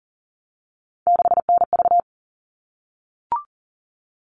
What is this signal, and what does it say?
Morse code, heard as a keyed tone, about 40 wpm, 700 Hz. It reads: 6NV
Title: All Cut Numbers - 3-Digit 40wpm